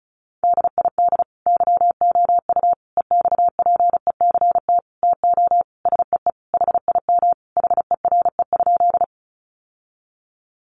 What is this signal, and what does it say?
Morse code, heard as a keyed tone, about 34 wpm, 700 Hz. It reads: DID YOU EXPECT TO SEE HIM HERE?
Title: Sentences from Top 1100 Words 35wpm